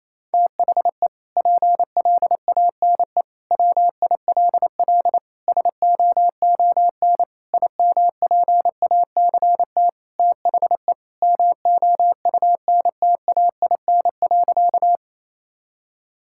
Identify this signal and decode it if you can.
Morse code, heard as a keyed tone, about 28 wpm, 700 Hz. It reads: THE PLANE WILL SOON IMPACT THE MOUNTAIN.
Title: Sentences from Top 1000 Words - Review Part 6 - 28wpm